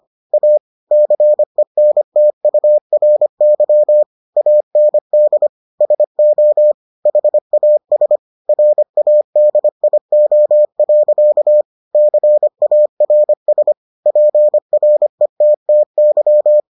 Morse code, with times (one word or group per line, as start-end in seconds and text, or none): 0.3-0.6 A
0.9-4.0 CENTURY
4.4-5.5 AND
5.8-6.7 SO
7.1-8.2 HAS
8.5-11.6 RADIO.
11.9-13.7 CARS
14.1-16.6 PRETTY